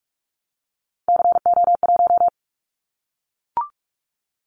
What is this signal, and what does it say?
Morse code, heard as a keyed tone, about 44 wpm, 700 Hz. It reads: CO1
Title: Callsign Prefixes for Top 100 DXCC - 45wpm